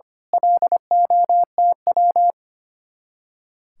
Morse code, heard as a keyed tone, about 28 wpm, 700 Hz. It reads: LOTW
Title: Speed-Racing Common QSO Phrases - 25wpm